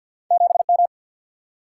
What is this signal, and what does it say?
Morse code, heard as a keyed tone, about 50 wpm, 700 Hz. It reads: 7 M
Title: Single-Letter Single-Number 50wpm